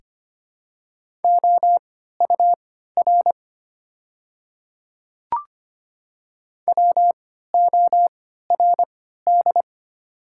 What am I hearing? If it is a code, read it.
Morse code, heard as a keyed tone, about 25 wpm, spaced as for 14 wpm, 690 Hz. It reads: OUR WORD
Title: 3x Character Spacing - Top 200 Words - 25wpm